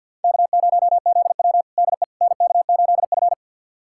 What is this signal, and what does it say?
Morse code, heard as a keyed tone, about 46 wpm, 690 Hz. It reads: K0ZW DE NK8F
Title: USA Callsign DE Callsign - 4 Characters - 50wpm